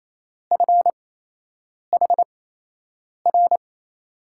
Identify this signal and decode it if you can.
Morse code, heard as a keyed tone, about 28 wpm, spaced as for 8 wpm, 700 Hz. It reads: FHR
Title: USA Call Signs - 6 Characters - 8x Character Spacing - 28wpm